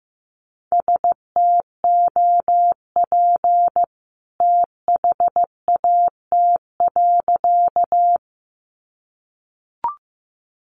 Morse code, heard as a keyed tone, about 15 wpm, 700 Hz. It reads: STOP THAT.
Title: A Sound Of Thunder 15wpm